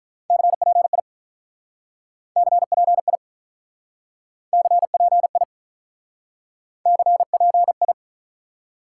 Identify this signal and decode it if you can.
Morse code, heard as a keyed tone, about 46 wpm, 700 Hz. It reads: CPI CPI CPI CPI